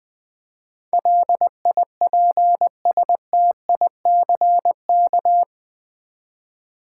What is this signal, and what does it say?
Morse code, heard as a keyed tone, about 20 wpm, 700 Hz. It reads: LIPSTICK
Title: Easy Words with Hints - 20wpm